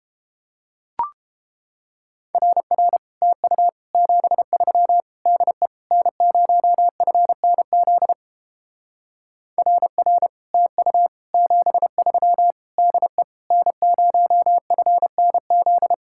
Morse code, heard as a keyed tone, about 30 wpm, 700 Hz. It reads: RR TU 73 DE N0FNZ RR TU 73 DE N0FNZ